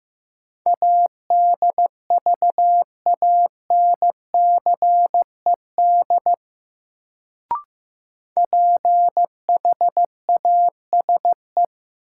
Morse code, heard as a keyed tone, about 15 wpm, 700 Hz. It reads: ADVANCED PHASE
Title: Sentences from Top 1200 Words - New Word Drill - 15wpm